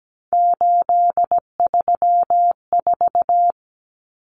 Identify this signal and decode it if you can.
Morse code, heard as a keyed tone, about 17 wpm, 700 Hz. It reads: 834
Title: Common Cut Numbers - 3-Digit 17wpm